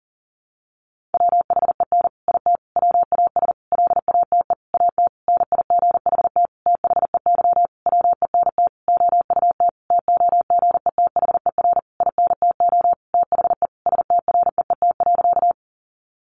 Morse code, heard as a keyed tone, about 40 wpm, 700 Hz. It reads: WHEN IT WAS LATE AT NIGHT THEY WENT OUT TOGETHER INTO THE STREET.
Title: Sentences from Top 600 Words 40wpm